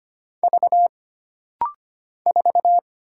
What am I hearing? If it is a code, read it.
Morse code, heard as a keyed tone, about 25 wpm, 700 Hz. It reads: V 4
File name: Single Character (Letter:Number:Punctuation:Prosign) - Rapid-Fire - 25wpm